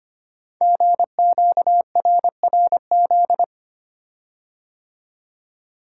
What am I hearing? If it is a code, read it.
Morse code, heard as a keyed tone, about 25 wpm, 690 Hz. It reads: GQRRZ
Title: Code Groups - 5 Letters - 25wpm